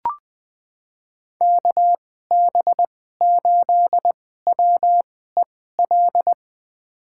Morse code, heard as a keyed tone, about 20 wpm, spaced as for 15 wpm, 700 Hz. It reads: KB8WEL
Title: USA Call Signs - 6 Characters - 2x Character Spacing - 20wpm